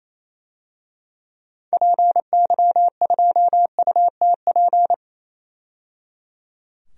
Morse code, heard as a keyed tone, about 28 wpm, 700 Hz. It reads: PY2UTP
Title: DXCC-100 Call Signs - 6 Characters - 28wpm